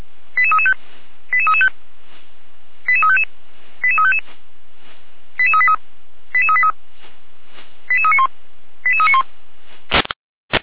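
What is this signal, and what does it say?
Five-tone selective call (ZVEI), heard as quick runs of five tones, tones of about 70 ms. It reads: 88335 88335 88355 88355 88383 88383 88391 88391
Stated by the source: ZVEI five-tone selective call received by SDR.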